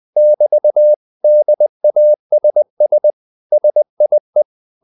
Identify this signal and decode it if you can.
Morse code, heard as a keyed tone, about 20 wpm, 600 Hz. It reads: <BT> DASS SIE